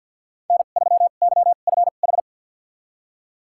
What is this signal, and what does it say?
Morse code, heard as a keyed tone, about 52 wpm, spaced as for 36 wpm, 700 Hz. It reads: N3YFH
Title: USA Call Signs - 5 Characters - 2x Character Spacing - 50wpm